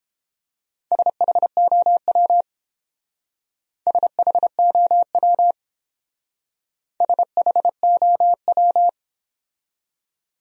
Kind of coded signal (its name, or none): Morse code